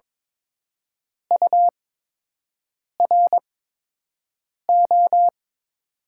Morse code, heard as a keyed tone, about 22 wpm, spaced as for 6 wpm, 700 Hz. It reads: URO
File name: Top 90 QSO Elements - 8x Character Spacing - 22wpm